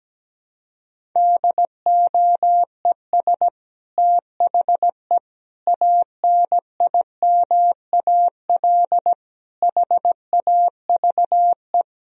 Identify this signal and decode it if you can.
Morse code, heard as a keyed tone, about 17 wpm, 700 Hz. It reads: DOES THE ANIMAL HAVE
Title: Sentences from Top 1000 Words - Review Part 1 - 17wpm